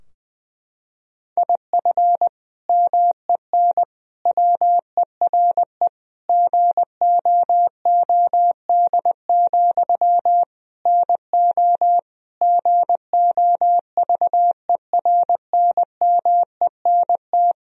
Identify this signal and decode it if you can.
Morse code, heard as a keyed tone, about 20 wpm, 700 Hz. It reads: IF MEN WERE GOOD, NO GOVERNMENT
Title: Sentences from Top 1000 Words - Review Part 3 - 20wpm